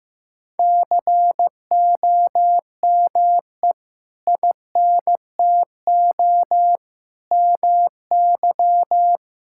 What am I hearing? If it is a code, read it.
Morse code, heard as a keyed tone, about 15 wpm, 700 Hz. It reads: COME INTO MY
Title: Sentences from Top 1000 Words - Review Part 6 - 15wpm